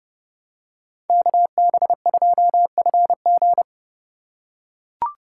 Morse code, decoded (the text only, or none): KB2FG